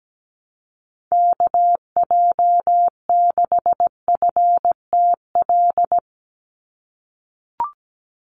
Morse code, heard as a keyed tone, about 17 wpm, 700 Hz. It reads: KJ6FTL
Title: USA Call Signs 17wpm